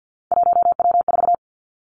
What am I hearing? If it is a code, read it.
Morse code, heard as a keyed tone, about 50 wpm, 700 Hz. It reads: 1W4
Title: Three Challenging Letter-Number Combinations 50wpm